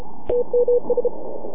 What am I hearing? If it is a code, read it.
Morse code, heard as a keyed tone, about 33 wpm, 500 Hz. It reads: TMS
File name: NAQPCW-8-2009 / W1UJ NAQP CW 09 1800Z